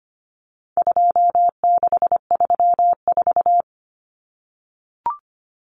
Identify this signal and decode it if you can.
Morse code, heard as a keyed tone, about 25 wpm, 700 Hz. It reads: 2634
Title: Numbers - 4-Digit 25wpm